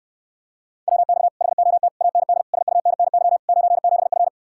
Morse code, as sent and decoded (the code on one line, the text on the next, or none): .-- -..- / .... --- - / .- -. -.. / ... ..- -. -. -.-- / ----. --... ..-.
WX HOT AND SUNNY 97F